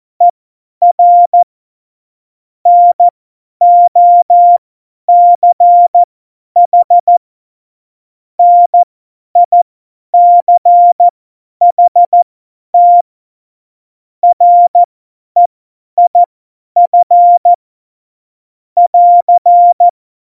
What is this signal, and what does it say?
Morse code, heard as a keyed tone, about 14 wpm, spaced as for 10 wpm, 700 Hz. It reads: ER NOCH NICHT REIF <AR>
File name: Morsekurs / Witze / Witz 04. September 2018, 50 F70